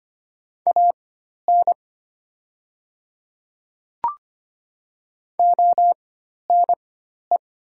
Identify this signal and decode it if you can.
Morse code, heard as a keyed tone, about 25 wpm, spaced as for 12 wpm, 700 Hz. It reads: AN ONE
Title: 4x Character Spacing - Top 100 Words - 25wpm